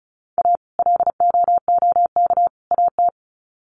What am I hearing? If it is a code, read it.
Morse code, heard as a keyed tone, about 34 wpm, 700 Hz. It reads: A LOOK AT
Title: Sets of 3 Words 35wpm